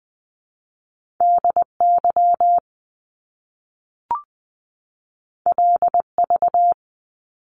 Morse code, heard as a keyed tone, about 20 wpm, 700 Hz. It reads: DY LV
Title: Top Two Letter Combinations in Words 20wpm